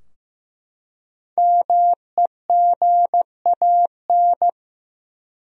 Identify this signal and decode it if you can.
Morse code, heard as a keyed tone, about 15 wpm, 700 Hz. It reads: MEGAN
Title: USA Common Names - 15wpm